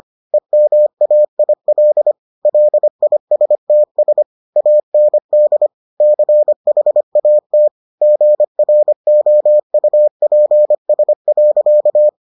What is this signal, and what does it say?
Morse code, heard as a keyed tone, about 25 wpm, 600 Hz. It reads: EMAIL LISTS AND CHAT GROUPS.